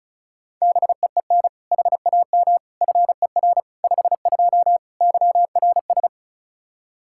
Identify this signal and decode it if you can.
Morse code, heard as a keyed tone, about 34 wpm, 700 Hz. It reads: BEEN HAM FER 52 YRS
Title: Practice QSO - Exchange 3 - More Info - 35wpm